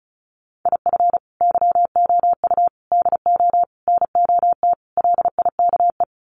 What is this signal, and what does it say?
Morse code, heard as a keyed tone, about 34 wpm, 700 Hz. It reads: IF YOU DO NOT LIKE
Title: Sets of 5 Words 35wpm